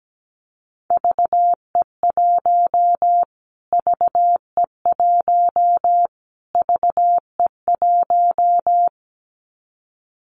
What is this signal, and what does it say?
Morse code, heard as a keyed tone, about 17 wpm, 700 Hz. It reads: VE1 VE1 VE1
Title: Callsign Prefixes for Top 25 DXCC - Repeated 3x - 17wpm